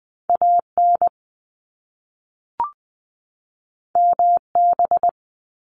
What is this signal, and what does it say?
Morse code, heard as a keyed tone, about 20 wpm, 700 Hz. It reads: AN MB
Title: Top Two Letter Combinations in Words 20wpm